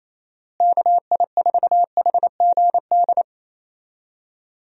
Morse code, heard as a keyed tone, about 28 wpm, 700 Hz. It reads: KI4HGD
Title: USA Call Signs - 6 Characters - Repeated 2x - 28wpm